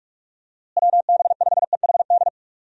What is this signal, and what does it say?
Morse code, heard as a keyed tone, about 46 wpm, 700 Hz. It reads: WB5EHD
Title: USA Call Signs - 6 Characters - Repeated 2x - 45wpm